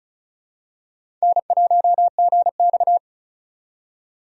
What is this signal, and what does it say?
Morse code, heard as a keyed tone, about 34 wpm, 700 Hz. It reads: N1GX
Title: USA Call Signs - 4 Characters - 35wpm